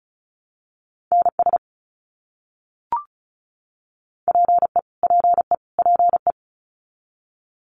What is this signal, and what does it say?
Morse code, heard as a keyed tone, about 34 wpm, 700 Hz. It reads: NS PE PE PE
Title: Canadian Provinces and Territories - Repeated 3x - 35wpm